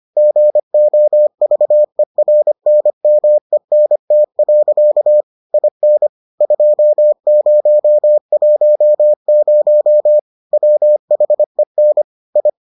Morse code, heard as a keyed tone, about 25 wpm, 600 Hz. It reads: GOVERNMENT. IN 2010 WHEN I